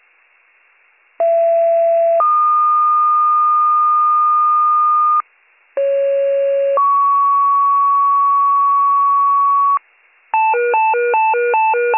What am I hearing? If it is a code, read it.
Two-tone page, heard as a one-second tone, then a three-second tone, about 670 then 1220 Hz, about 550 then 1120 Hz. It is heard from 1.2 to 9.8 s.